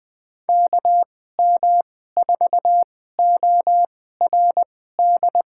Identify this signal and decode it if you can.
Morse code, heard as a keyed tone, about 21 wpm, spaced as for 15 wpm, 700 Hz. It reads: KM4ORD
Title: USA Call Signs - 6 Characters - 2x Character Spacing - 20wpm